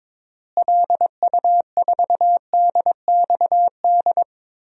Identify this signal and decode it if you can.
Morse code, heard as a keyed tone, about 22 wpm, 700 Hz. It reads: LU4DXD
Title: DXCC-100 Call Signs - 6 Characters - Repeated 3x - 22wpm